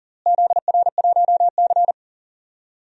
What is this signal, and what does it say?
Morse code, heard as a keyed tone, about 37 wpm, 700 Hz. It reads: ZR1C
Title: DXCC-100 Call Signs - 4 Characters - Repeated 3x - 40wpm